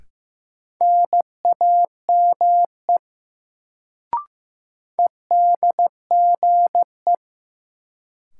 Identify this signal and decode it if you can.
Morse code, heard as a keyed tone, about 15 wpm, 700 Hz. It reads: NAME EDGE
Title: Sets of 3 Words - New Word Drill - 15wpm